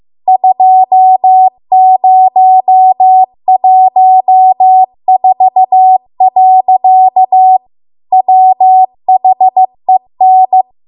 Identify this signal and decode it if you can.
Morse code, heard as a keyed tone, about 15 wpm, 750 Hz. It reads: 2014. WHEN